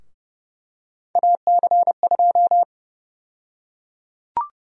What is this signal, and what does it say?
Morse code, heard as a keyed tone, about 30 wpm, 700 Hz. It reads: AC2
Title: USA Call Sign Prefixes - Repeated 2x - 30wpm